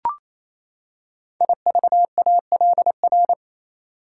Morse code, heard as a keyed tone, about 27 wpm, 700 Hz. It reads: IVALR
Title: Code Groups - 5 Letters - 28wpm